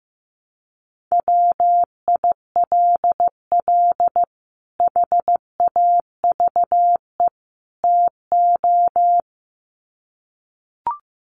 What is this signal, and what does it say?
Morse code, heard as a keyed tone, about 15 wpm, 700 Hz. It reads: WILL HAVE TO
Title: Sets of 3 Words 15wpm